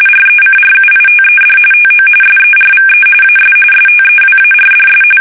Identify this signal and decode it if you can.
RTTY, about 45.45 baud. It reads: RECRUITED TO SERVE AS STEWARDS
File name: fsk850